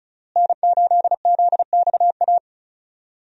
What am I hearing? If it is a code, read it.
Morse code, heard as a keyed tone, about 36 wpm, 700 Hz. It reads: N8ZXA
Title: USA Call Signs - 5 Characters - Repeated 3x - 35wpm